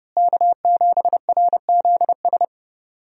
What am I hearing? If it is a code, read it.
Morse code, heard as a keyed tone, about 30 wpm, 700 Hz. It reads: K7RZS
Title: USA Call Signs - 5 Characters - Repeated 3x - 30wpm